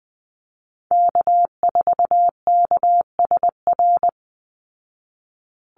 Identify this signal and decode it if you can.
Morse code, heard as a keyed tone, about 20 wpm, 700 Hz. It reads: K4KSR
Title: Call Signs Repeated 3x - 20wpm